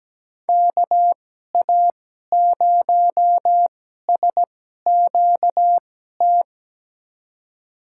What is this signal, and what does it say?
Morse code, heard as a keyed tone, about 17 wpm, spaced as for 12 wpm, 700 Hz. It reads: KA0SQT
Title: USA Call Signs - 2x Character Spacing - 17wpm